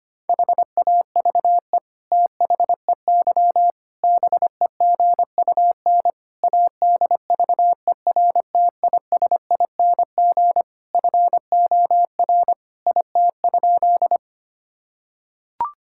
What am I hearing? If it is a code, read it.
Morse code, heard as a keyed tone, about 25 wpm, 700 Hz. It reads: HAVE THEY BEGUN ADVERTISING FOR IT?
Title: Sentences from Top 1000 Words - Review Part 6 - 25wpm